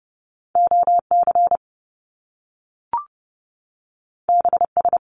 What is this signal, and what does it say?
Morse code, heard as a keyed tone, about 30 wpm, 700 Hz. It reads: OC BS
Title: Top Two Letter Combinations in Words 30wpm